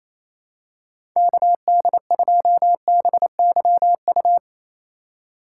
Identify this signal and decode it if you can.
Morse code, heard as a keyed tone, about 27 wpm, 700 Hz. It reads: KD2BYU
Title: USA Call Signs - 6 Characters - 4x Character Spacing - 28wpm